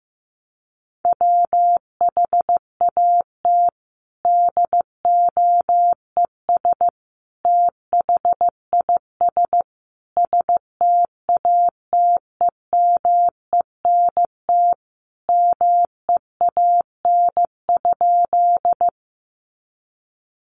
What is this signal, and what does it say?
Morse code, heard as a keyed tone, about 15 wpm, 700 Hz. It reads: WHAT DOES THIS STATEMENT MEAN?
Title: Sentences from Top 1200 Words 15wpm